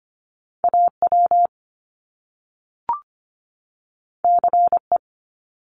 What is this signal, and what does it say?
Morse code, heard as a keyed tone, about 25 wpm, 700 Hz. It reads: A W C E
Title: Top Two Letter Combinations in Words 25wpm